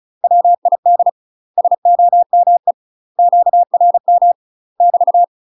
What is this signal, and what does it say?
Morse code, heard as a keyed tone, about 36 wpm, 700 Hz. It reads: WID SOME QRM <BT>
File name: UK MASTER QSOs TRAINING / 35WPM / UK MASTER QSOs 0013